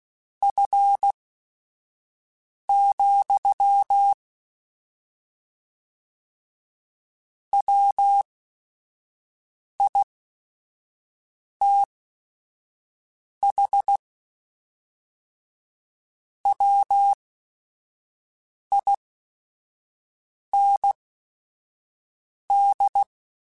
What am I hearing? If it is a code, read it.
Morse code, heard as a keyed tone, about 16 wpm, spaced as for 5 wpm, 800 Hz. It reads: F, WITH WIND
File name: Weather Sets / Weather Sets-mp3s / Weather 5 22 @5